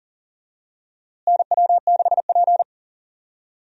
Morse code, heard as a keyed tone, about 40 wpm, 700 Hz. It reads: NW6P
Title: USA Call Signs - 4 Characters - Repeated 3x - 40wpm